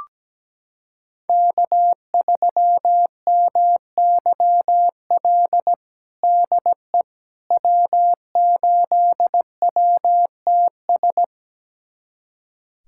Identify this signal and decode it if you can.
Morse code, heard as a keyed tone, about 17 wpm, 700 Hz. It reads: K3MYL DE W8WTS
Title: USA Callsign DE Callsign - 5 Characters - 17wpm